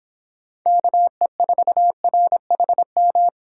Morse code, heard as a keyed tone, about 25 wpm, 700 Hz. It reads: KE4RHM